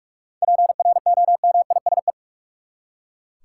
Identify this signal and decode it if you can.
Morse code, heard as a keyed tone, about 42 wpm, 700 Hz. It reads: PROMISE
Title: Seven Letter Words - 45wpm